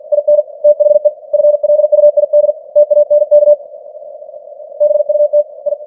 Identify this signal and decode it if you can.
Morse code, heard as a keyed tone, about 50 wpm, 590 Hz. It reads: IN THE FLUID TANK BUT I
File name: archived / CWBros